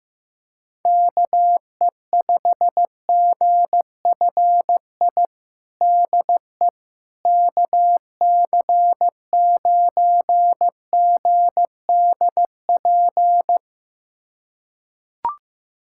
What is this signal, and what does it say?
Morse code, heard as a keyed tone, about 15 wpm, 700 Hz. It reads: KE5GFI DE KC9GDP
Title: USA Callsign DE Callsign 15wpm